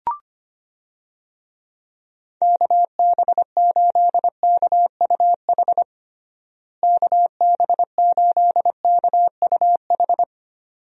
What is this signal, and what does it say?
Morse code, heard as a keyed tone, about 25 wpm, 700 Hz. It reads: KB8KUH KB8KUH
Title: USA Call Signs - 6 Characters - Repeated 2x - 25wpm